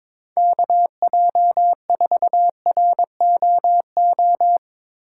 Morse code, heard as a keyed tone, about 22 wpm, 700 Hz. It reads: KJ4ROO
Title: USA Call Signs - 6 Characters - 2x Character Spacing - 22wpm